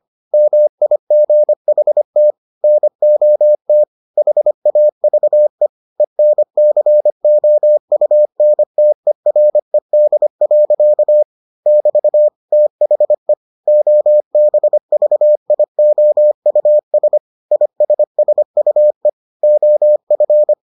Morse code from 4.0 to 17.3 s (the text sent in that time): HAVE ENCOUNTERED. <BT> THE OBVIOUS